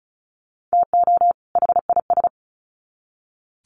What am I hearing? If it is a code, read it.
Morse code, heard as a keyed tone, about 34 wpm, 700 Hz. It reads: TO HIS
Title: Sets of 2 Words 35wpm